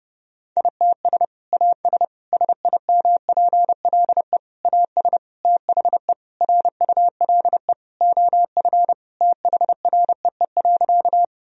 Morse code, heard as a keyed tone, about 30 wpm, 700 Hz. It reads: ITS AS SIMPLE AS THE RULE OF THREE.
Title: Sentences from Top 1000 Words - Review Part 2 - 30wpm